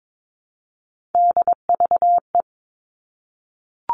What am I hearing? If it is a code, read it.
Morse code, heard as a keyed tone, about 22 wpm, 700 Hz. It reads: DVE
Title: All Cut Numbers - 3-Digit 22wpm